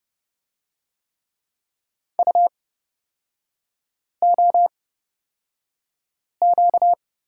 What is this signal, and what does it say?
Morse code, heard as a keyed tone, about 31 wpm, spaced as for 5 wpm, 700 Hz. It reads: UOQ
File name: Lesson 38e - Letter q - Three Characters 30wpm